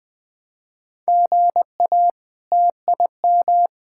Morse code, heard as a keyed tone, about 21 wpm, 700 Hz. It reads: GA TIM
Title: K1USN SST QSO Practice - 20wpm